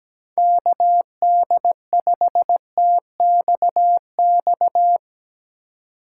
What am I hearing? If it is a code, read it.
Morse code, heard as a keyed tone, about 17 wpm, 700 Hz. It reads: KD5TXX
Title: K1USN SST QSO Practice - 17wpm